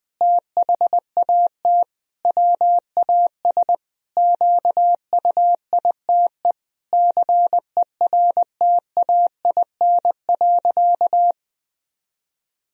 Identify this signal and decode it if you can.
Morse code, heard as a keyed tone, about 20 wpm, 700 Hz. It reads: THAT WAS QUITE CERTAIN.
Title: Sentences from Top 1000 Words - Review Part 2 - 20wpm